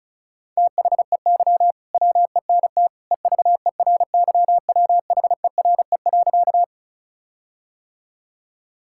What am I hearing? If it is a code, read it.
Morse code, heard as a keyed tone, about 36 wpm, 700 Hz. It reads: THEY WENT EVERYWHERE.
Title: Sentences from Top 1600 Words 35wpm